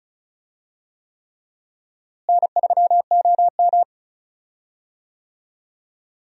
Morse code, heard as a keyed tone, about 36 wpm, 700 Hz. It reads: N3OM
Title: USA Call Signs - 4 Characters - Repeated 3x - 35wpm